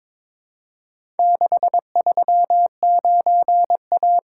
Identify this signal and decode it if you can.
Morse code, heard as a keyed tone, about 23 wpm, 700 Hz. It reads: 639A
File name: practice / 22wpm / Four Letter-Number Combinations - 22wpm